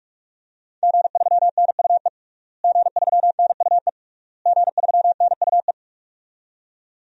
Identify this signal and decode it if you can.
Morse code, heard as a keyed tone, about 41 wpm, 700 Hz. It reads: G3NUE G3NUE G3NUE
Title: DXCC-100 Call Signs - 5 Characters - Repeated 3x - 45wpm